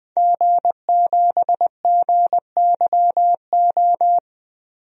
Morse code, heard as a keyed tone, about 20 wpm, 700 Hz. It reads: G7GYO
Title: DXCC-100 Call Signs - 5 Characters - Repeated 2x - 20wpm